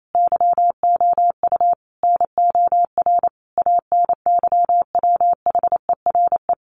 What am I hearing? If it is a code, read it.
Morse code, heard as a keyed tone, about 28 wpm, 700 Hz. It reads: YOU NOR ANYWHERE
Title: Sentences from Top 1500 Words 28wpm